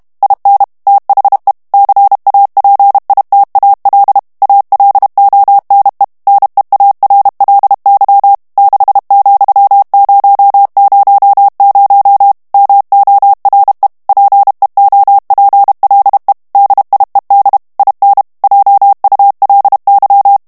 Morse code, heard as a keyed tone, about 32 wpm, 800 Hz. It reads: IN THE CAPITAL ALONE NEARLY 6,000 MORE PEOPLE DIED IN JULY
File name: ANGLAIS texte clair 2 vitesse 30 mots minute